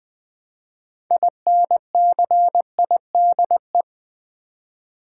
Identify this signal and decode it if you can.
Morse code, heard as a keyed tone, about 20 wpm, 700 Hz. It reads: INCIDE
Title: Build A Word - 3 Letters At A Time - 8-Letter Words - 20wpm